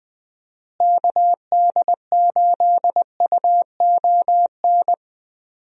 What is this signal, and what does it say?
Morse code, heard as a keyed tone, about 20 wpm, 700 Hz. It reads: KD8UON
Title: USA Call Signs - 6 Characters - Repeated 2x - 20wpm